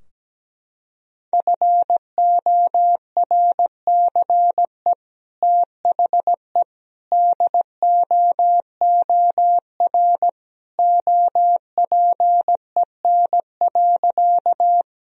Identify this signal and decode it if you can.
Morse code, heard as a keyed tone, about 17 wpm, 700 Hz. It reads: FORCE THE DOOR OPEN.
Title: Sentences from Top 1000 Words - Review Part 3 - 17wpm